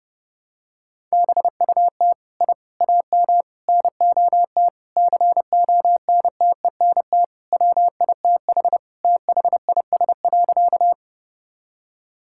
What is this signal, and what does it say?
Morse code, heard as a keyed tone, about 30 wpm, 700 Hz. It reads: BUT I AM NOT CONTENT WITH THIS.
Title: Sentences from Top 1000 Words - Review Part 6 - 30wpm